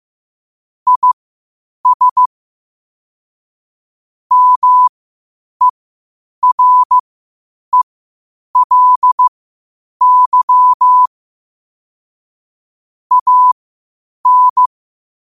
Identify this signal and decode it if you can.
Morse code, heard as a keyed tone, about 15 wpm, spaced as for 8 wpm, 1000 Hz. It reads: IS MERELY AN
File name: qotdfiles / 8wpm